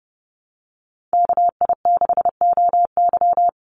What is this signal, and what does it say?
Morse code, heard as a keyed tone, about 30 wpm, 700 Hz. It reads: KI6OY
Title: Call Signs Repeated 3x - 30wpm